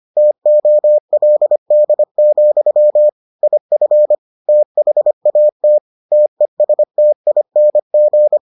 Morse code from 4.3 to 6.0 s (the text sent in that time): THAT